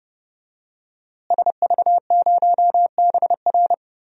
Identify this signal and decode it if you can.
Morse code, heard as a keyed tone, about 32 wpm, 700 Hz. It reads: SV0BR
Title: DXCC-100 Call Signs - 5 Characters - Repeated 3x - 30wpm